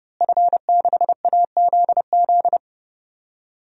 Morse code, heard as a keyed tone, about 30 wpm, 700 Hz. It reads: F6AZZ
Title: DXCC-100 Call Signs - 5 Characters - Repeated 3x - 30wpm